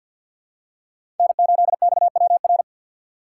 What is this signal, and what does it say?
Morse code, heard as a keyed tone, about 47 wpm, 700 Hz. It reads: N8XWR
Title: USA Call Signs - 5 Characters - 6x Character Spacing - 50wpm